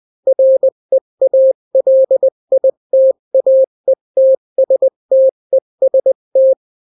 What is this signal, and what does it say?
Morse code, heard as a keyed tone, about 20 wpm, 530 Hz. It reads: REALITAETSTEST
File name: daily / 18wpm